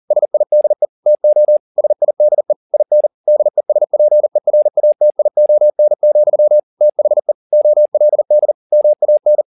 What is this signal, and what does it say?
Morse code, heard as a keyed tone, about 40 wpm, 600 Hz. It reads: SIDE TO SIDE IN DESPERATION, THE OLD MAN
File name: the old man and the sea 2 40wpm